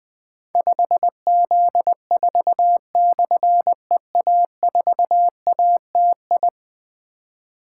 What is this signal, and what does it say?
Morse code, heard as a keyed tone, about 20 wpm, 700 Hz. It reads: 5Z4/EA4ATI
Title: Difficult Call Signs 20wpm